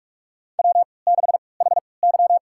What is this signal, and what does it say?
Morse code, heard as a keyed tone, about 46 wpm, spaced as for 26 wpm, 700 Hz. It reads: W6HY
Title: USA Call Signs - 4 Characters - 3x Character Spacing - 45wpm